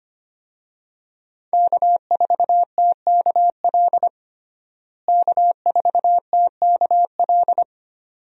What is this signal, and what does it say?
Morse code, heard as a keyed tone, about 25 wpm, 700 Hz. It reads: K4TKL K4TKL
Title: USA Call Signs - 5 Characters - Repeated 2x - 25wpm